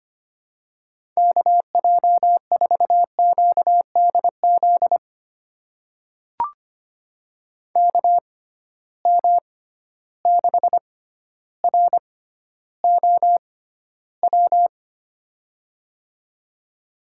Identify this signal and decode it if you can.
Morse code, heard as a keyed tone, about 25 wpm, spaced as for 13 wpm, 700 Hz. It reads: KJ4QDZ KM6ROW
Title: USA Call Signs - 6 Characters - 6x Character Spacing - 25wpm